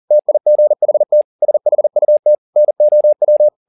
Morse code, heard as a keyed tone, about 40 wpm, 600 Hz. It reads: TIGHT SHUT NOW
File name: the old man and the sea 2 40wpm